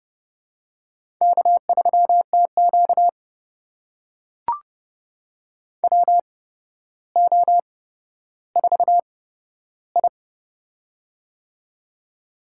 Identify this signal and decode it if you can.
Morse code, heard as a keyed tone, about 30 wpm, spaced as for 13 wpm, 700 Hz. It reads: K3TQ WO4I
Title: USA Call Signs - 4 Characters - 8x Character Spacing - 30wpm